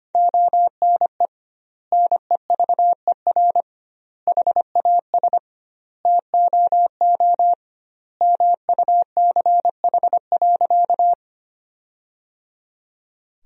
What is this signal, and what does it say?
Morse code, heard as a keyed tone, about 25 wpm, 700 Hz. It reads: ONE NEVER HAS TOO MUCH.
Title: 2x Word-Spacing - Sentences from Top 300 Words - 25wpm